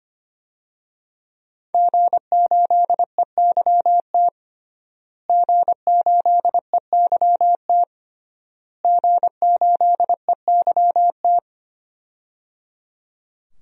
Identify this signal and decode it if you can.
Morse code, heard as a keyed tone, about 25 wpm, 700 Hz. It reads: G8EYT G8EYT G8EYT
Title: DXCC-100 Call Signs - 5 Characters - Repeated 3x - 25wpm